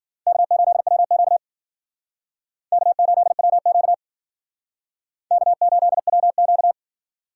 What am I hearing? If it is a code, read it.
Morse code, heard as a keyed tone, about 52 wpm, 700 Hz. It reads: K8WQ K8WQ K8WQ